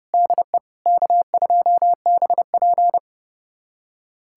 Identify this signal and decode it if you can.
Morse code, heard as a keyed tone, about 31 wpm, 700 Hz. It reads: DE K2BP
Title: USA Callsign DE Callsign - 4 Characters - 30wpm